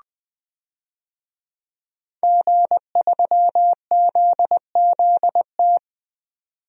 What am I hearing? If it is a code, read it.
Morse code, heard as a keyed tone, about 20 wpm, 700 Hz. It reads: G3ZZT
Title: DXCC-100 Call Signs - 5 Characters - 20wpm